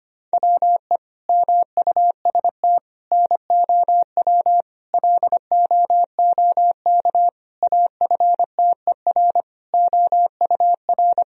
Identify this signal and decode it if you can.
Morse code, heard as a keyed tone, about 25 wpm, 700 Hz. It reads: WE MUST NOW LOOK AFTER OUR
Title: Sentences from Top 1000 Words - Review Part 4 - 25wpm